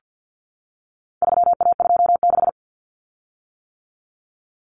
Morse code, heard as a keyed tone, about 50 wpm, 700 Hz. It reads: 3A26
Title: Common Cut Numbers - 4-Digit 50wpm